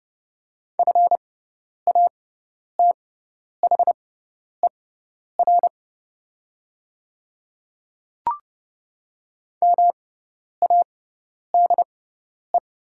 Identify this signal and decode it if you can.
Morse code, heard as a keyed tone, about 31 wpm, spaced as for 10 wpm, 700 Hz. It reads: FATHER MADE
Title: 6x Character Spacing - Top 200 Words - 30wpm